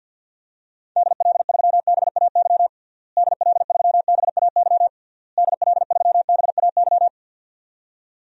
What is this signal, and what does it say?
Morse code, heard as a keyed tone, about 48 wpm, 700 Hz. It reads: DL3BAY DL3BAY DL3BAY
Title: DXCC-100 Call Signs - 6 Characters - Repeated 3x - 50wpm